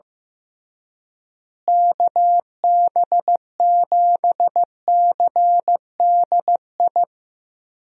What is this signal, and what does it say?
Morse code, heard as a keyed tone, about 15 wpm, 700 Hz. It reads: KB7CDI
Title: USA Call Signs - 6 Characters - Repeated 3x - 15wpm